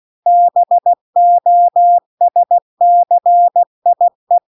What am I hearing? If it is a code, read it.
Morse code, heard as a keyed tone, about 16 wpm, 700 Hz. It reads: BOSCIE